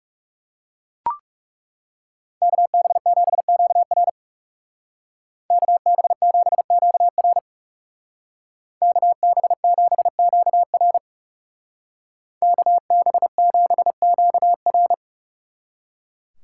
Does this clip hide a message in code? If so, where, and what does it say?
Yes: Morse code from 2.4 to 14.9 s, heard as a keyed tone, about 39 wpm, 700 Hz. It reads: KB7QR KB7QR KB7QR KB7QR